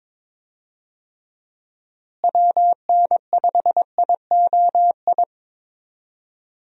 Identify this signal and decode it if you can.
Morse code, heard as a keyed tone, about 22 wpm, 700 Hz. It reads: WN5IOI
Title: USA Call Signs - 6 Characters - Repeated 2x - 22wpm